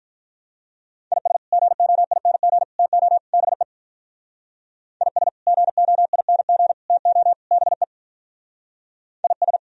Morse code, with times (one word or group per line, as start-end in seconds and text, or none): 1.1-1.4 IS
1.5-2.6 GOING
2.8-3.2 TO
3.3-3.6 BE
5.0-5.3 IS
5.5-6.7 GOING
6.9-7.3 TO
7.5-7.8 BE
9.2-9.6 IS